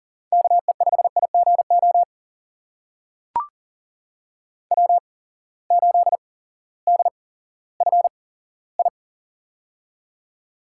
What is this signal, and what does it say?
Morse code, heard as a keyed tone, about 40 wpm, spaced as for 18 wpm, 700 Hz. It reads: KE5IGO W8DFI
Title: USA Call Signs - 8x Character Spacing - 40wpm